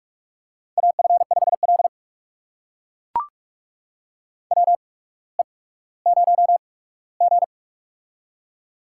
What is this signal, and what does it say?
Morse code, heard as a keyed tone, about 44 wpm, spaced as for 19 wpm, 690 Hz. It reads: AF5L WE0G
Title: USA Call Signs - 4 Characters - 8x Character Spacing - 45wpm